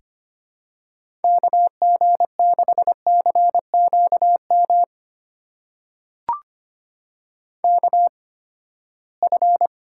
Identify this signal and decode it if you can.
Morse code, heard as a keyed tone, about 25 wpm, spaced as for 17 wpm, 700 Hz. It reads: KG6CQM KF
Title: USA Call Signs - 8x Character Spacing - 25wpm